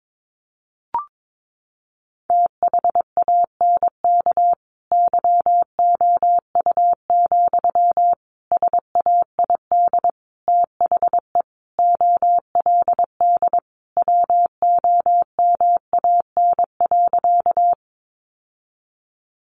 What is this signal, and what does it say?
Morse code, heard as a keyed tone, about 22 wpm, 700 Hz. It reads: THANK YOU, SAID THE OLD WOMAN.
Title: Sentences from Top 1100 Words 22wpm